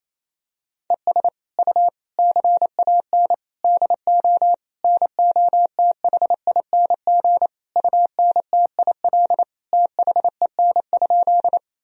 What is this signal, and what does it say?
Morse code, heard as a keyed tone, about 28 wpm, 700 Hz. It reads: ES U CAN DO NOTHING UNTIL THEN?
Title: Sentences from Top 500 Words and Abbreviations 28wpm